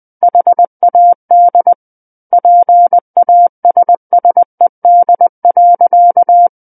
Morse code, heard as a keyed tone, about 20 wpm, 700 Hz. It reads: HAD PASSED.
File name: 20 WPM